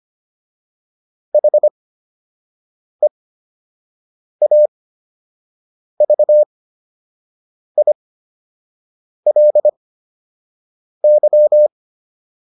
Morse code, heard as a keyed tone, about 25 wpm, spaced as for 6 wpm, 600 Hz. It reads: HEAVILY